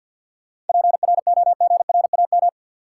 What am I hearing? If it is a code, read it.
Morse code, heard as a keyed tone, about 51 wpm, 700 Hz. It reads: PROGRAM
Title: Sentences from Top 1500 Words - New Word Drill Alt - 50wpm